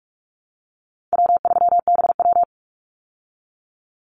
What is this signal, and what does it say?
Morse code, heard as a keyed tone, about 44 wpm, 700 Hz. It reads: W3BW
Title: Easy Call Signs Repeated 3x - 45wpm